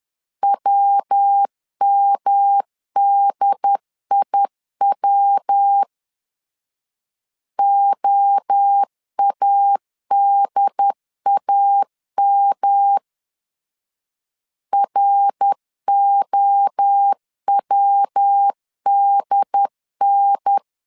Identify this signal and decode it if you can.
Morse code, heard as a keyed tone, about 10 wpm, 800 Hz. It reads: WMDIW OADAM ROWDN
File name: CW / DET / 1 3 1500C S B 10wpm 3335